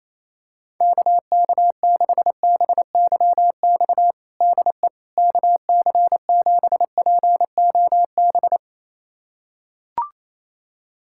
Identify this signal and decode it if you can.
Morse code, heard as a keyed tone, about 28 wpm, 700 Hz. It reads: KK6BYX DE KC7POB
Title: USA Callsign DE Callsign - 6 Characters - 28wpm